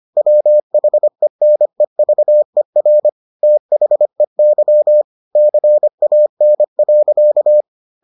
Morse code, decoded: WHENEVER THEY CAN.

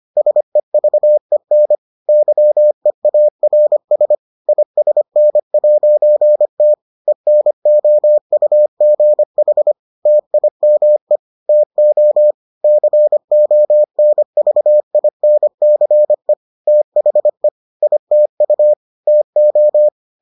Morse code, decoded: SEVEN YEARS ISN'T ENOUGH TIME TO CONVINCE THE ITU TO

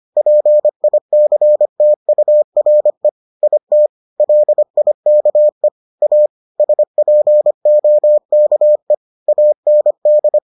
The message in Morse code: PICTURE IT LIKE A SPOKE AND

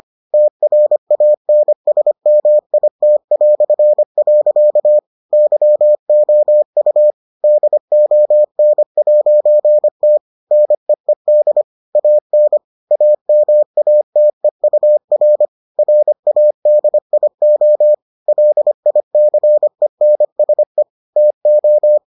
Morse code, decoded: TRANSMIT". YOU DON'T NEED AN AMATEUR RADIO LICENSE TO

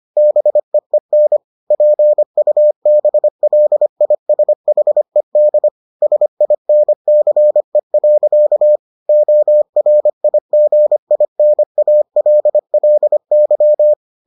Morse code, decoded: BEEN PUBLISHED SINCE. ORIGINALLY

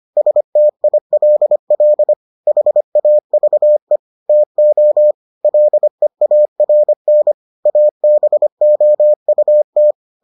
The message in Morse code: STILL HAVE TO LEARN ABOUT

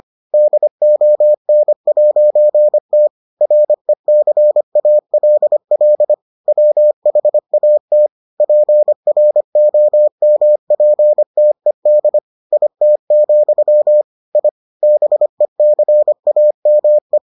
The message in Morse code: DON'T RECALL WHAT PROMPTED IT, I BECAME

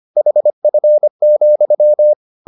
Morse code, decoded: HF,

